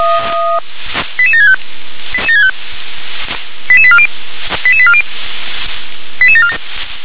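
Five-tone selective call (ZVEI), heard as quick runs of five tones, tones of about 70 ms. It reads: 88765 88840 88840 88747
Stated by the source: ZVEI five-tone selective call 88958 received by SDR.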